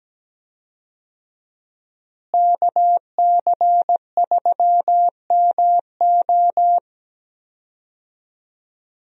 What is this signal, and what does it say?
Morse code, heard as a keyed tone, about 17 wpm, 700 Hz. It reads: KC3MO
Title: USA Call Signs - 5 Characters - Repeated 3x - 17wpm